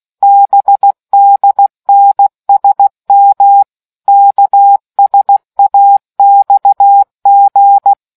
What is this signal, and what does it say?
Morse code, heard as a keyed tone, about 16 wpm, 800 Hz. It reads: BDNSM KSAXG